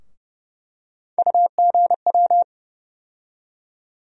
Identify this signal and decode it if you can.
Morse code, heard as a keyed tone, about 30 wpm, 700 Hz. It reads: UGW
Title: USA Call Sign Postfixes - 3 Letters 30wpm